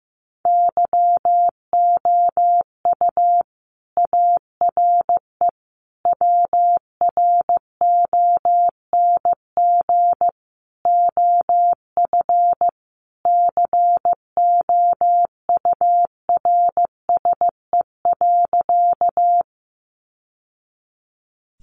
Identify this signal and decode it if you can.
Morse code, heard as a keyed tone, about 15 wpm, 700 Hz. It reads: YOU ARE WRONG OF COURSE.
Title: Sentences from Top 900 Words 15wpm